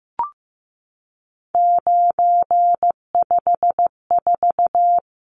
Morse code, heard as a keyed tone, about 15 wpm, 700 Hz. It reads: 954
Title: Numbers - 3-Digit 15wpm